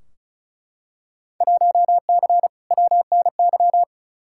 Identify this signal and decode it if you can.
Morse code, heard as a keyed tone, about 36 wpm, 700 Hz. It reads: 1C WNY
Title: Field Day Exchanges Encore - 35wpm